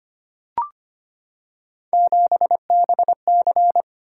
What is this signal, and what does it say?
Morse code, heard as a keyed tone, about 25 wpm, 700 Hz. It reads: 7BC
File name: Three Letter-Number Combinations - 25wpm